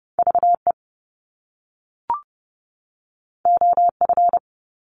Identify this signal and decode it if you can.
Morse code, heard as a keyed tone, about 30 wpm, 700 Hz. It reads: VE OF
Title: Top Two Letter Combinations in Words 30wpm